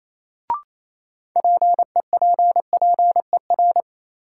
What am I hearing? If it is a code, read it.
Morse code, heard as a keyed tone, about 27 wpm, 700 Hz. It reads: PEPPER
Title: Sets of 5 Words - New Word Drill - 28wpm